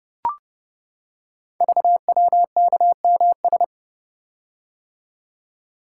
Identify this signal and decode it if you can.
Morse code, heard as a keyed tone, about 29 wpm, 700 Hz. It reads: VWKMS
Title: Code Groups - 5 Letters - 30wpm